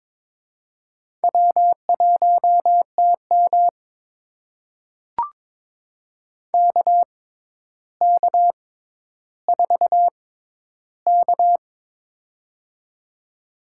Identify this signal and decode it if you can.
Morse code, heard as a keyed tone, about 22 wpm, spaced as for 11 wpm, 700 Hz. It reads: W1TM KK4K
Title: USA Call Signs - 4 Characters - 6x Character Spacing - 22wpm